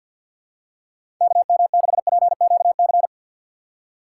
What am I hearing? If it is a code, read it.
Morse code, heard as a keyed tone, about 50 wpm, 700 Hz. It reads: KM6PQC
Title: USA Call Signs - 4x Character Spacing - 50wpm